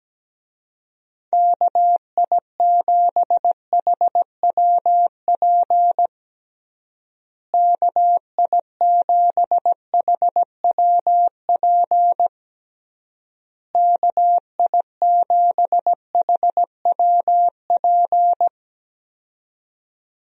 Morse code, heard as a keyed tone, about 17 wpm, 700 Hz. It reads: KI7HWP KI7HWP KI7HWP
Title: USA Call Signs - 6 Characters - Repeated 3x - 17wpm